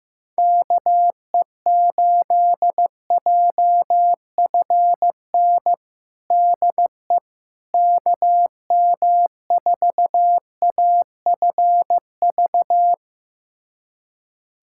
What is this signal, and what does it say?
Morse code, heard as a keyed tone, about 16 wpm, 700 Hz. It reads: KE8JFN DE KM4AFV
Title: USA Callsign DE Callsign - 6 Characters - 15wpm